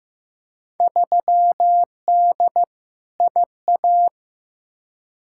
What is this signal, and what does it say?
Morse code, heard as a keyed tone, about 15 wpm, 700 Hz. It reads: 3D IA
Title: Field Day Exchanges Encore - 15wpm